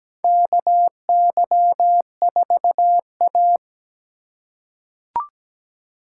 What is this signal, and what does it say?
Morse code, heard as a keyed tone, about 17 wpm, 700 Hz. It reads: KY4A
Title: USA Call Signs - 4 Characters - 4x Character Spacing - 17wpm